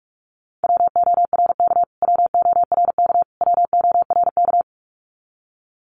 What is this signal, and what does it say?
Morse code, heard as a keyed tone, about 44 wpm, 700 Hz. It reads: WORK WORK WORK
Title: Top 200 Words Repeated 45wpm